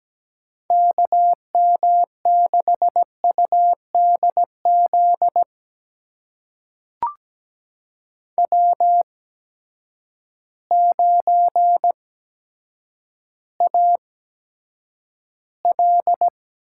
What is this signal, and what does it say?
Morse code, heard as a keyed tone, about 17 wpm, spaced as for 8 wpm, 700 Hz. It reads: KM6UDZ W9AL
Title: USA Call Signs - 8x Character Spacing - 17wpm